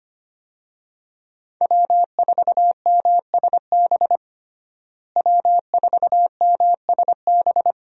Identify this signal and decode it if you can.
Morse code, heard as a keyed tone, about 25 wpm, 700 Hz. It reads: W4MSB W4MSB
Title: USA Call Signs - 5 Characters - Repeated 2x - 25wpm